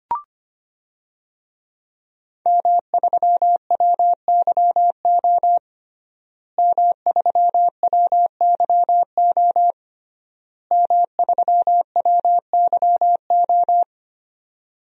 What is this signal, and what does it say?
Morse code, heard as a keyed tone, about 25 wpm, 700 Hz. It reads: M3WYO M3WYO M3WYO
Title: DXCC-100 Call Signs - 5 Characters - Repeated 3x - 25wpm